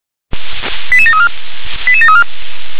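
Five-tone selective call (ZVEI), heard as quick runs of five tones, tones of about 70 ms. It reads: 88734 88734
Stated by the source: ZVEI five-tone selective call received by SDR.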